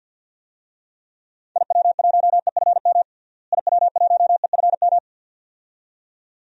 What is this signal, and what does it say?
Morse code, heard as a keyed tone, about 47 wpm, 700 Hz. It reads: IW1EFM IW1EFM
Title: DXCC-100 Call Signs - 6 Characters - Repeated 2x - 50wpm